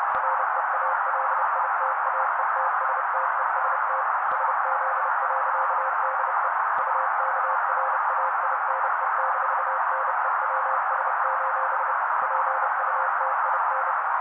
Morse code, heard as a keyed tone, about 30 wpm, 530 Hz. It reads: REALITAETSTEST SZ.DE UKRAINE-NEWS: PUTIN
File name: daily / 29wpm noise